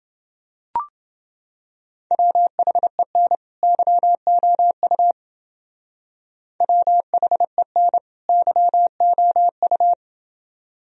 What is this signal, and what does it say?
Morse code, heard as a keyed tone, about 28 wpm, 700 Hz. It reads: WHEN YOU WHEN YOU